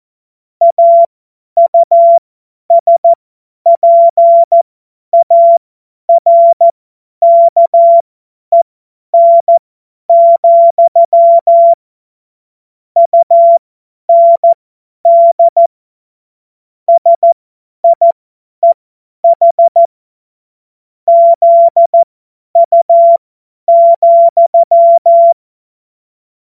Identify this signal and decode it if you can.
Morse code, heard as a keyed tone, about 14 wpm, spaced as for 10 wpm, 680 Hz. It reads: AUSPARKEN, UND SIEH ZU,
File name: Morsekurs / Witze / Witz 01. Februar 2022 50 F70